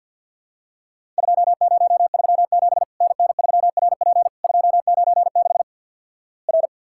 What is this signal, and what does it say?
Morse code, heard as a keyed tone, about 50 wpm, 700 Hz. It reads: 2037 NN3RP 296 R
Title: Worked All Europe DX Contest - QTC Exchange - 50wpm